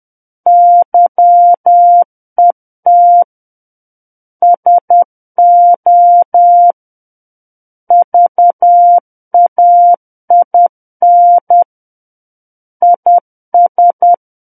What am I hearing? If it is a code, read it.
Morse code, heard as a keyed tone, about 10 wpm, 700 Hz. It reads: YET SO VAIN IS
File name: Book One / 01-Chapter 1 - 10 WPM